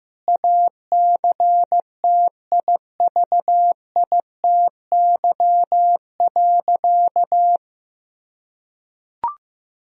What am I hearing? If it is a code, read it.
Morse code, heard as a keyed tone, about 15 wpm, 700 Hz. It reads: ACTIVITY.
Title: Sentences from Top 1000 Words - Review Part 4 - 15wpm